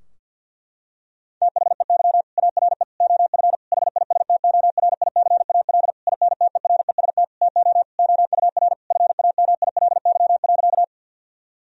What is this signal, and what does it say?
Morse code, heard as a keyed tone, about 50 wpm, 700 Hz. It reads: THEY ARE OF HISTORICAL INTEREST TO OUR FAMILY.
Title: Sentences from Top 1000 Words - Review Part 4 - 50wpm